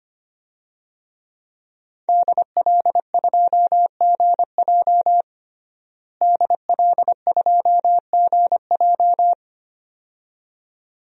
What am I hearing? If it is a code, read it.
Morse code, heard as a keyed tone, about 25 wpm, 700 Hz. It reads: DL2GJ DL2GJ
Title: DXCC-100 Call Signs - 5 Characters - Repeated 2x - 25wpm